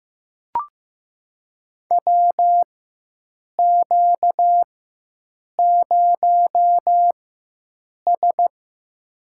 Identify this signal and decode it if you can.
Morse code, heard as a keyed tone, about 15 wpm, spaced as for 7 wpm, 700 Hz. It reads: WQ0S
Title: USA Call Signs - 4 Characters - 4x Character Spacing - 15wpm